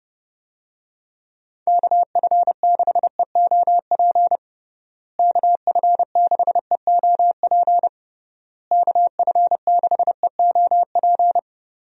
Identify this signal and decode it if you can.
Morse code, heard as a keyed tone, about 30 wpm, 700 Hz. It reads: KF6EOP KF6EOP KF6EOP
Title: USA Call Signs - 6 Characters - Repeated 3x - 30wpm